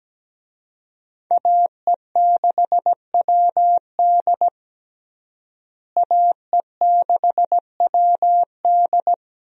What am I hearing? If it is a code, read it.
Morse code, heard as a keyed tone, about 17 wpm, 700 Hz. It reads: AE6WD AE6WD
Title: USA Call Signs - 5 Characters - Repeated 3x - 17wpm